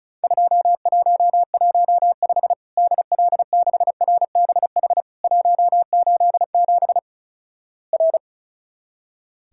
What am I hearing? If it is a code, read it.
Morse code, heard as a keyed tone, about 36 wpm, 700 Hz. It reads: 2115 DL6RBH 187 R
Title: Worked All Europe DX Contest - QTC Exchange - 35wpm